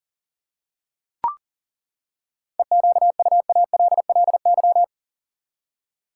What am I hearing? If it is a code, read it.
Morse code, heard as a keyed tone, about 40 wpm, 700 Hz. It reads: EQUALLY
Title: Seven Letter Words - 40wpm